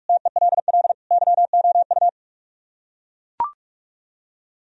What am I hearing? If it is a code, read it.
Morse code, heard as a keyed tone, about 41 wpm, 700 Hz. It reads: TELL YOU